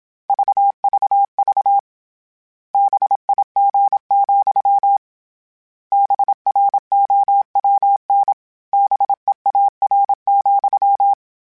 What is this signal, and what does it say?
Morse code, heard as a keyed tone, about 27 wpm, 800 Hz. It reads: VVV BIG, BROWN BEAR,